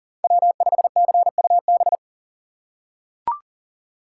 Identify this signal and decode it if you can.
Morse code, heard as a keyed tone, about 38 wpm, 700 Hz. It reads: W5CUB
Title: USA Call Signs - 5 Characters - 2x Character Spacing - 40wpm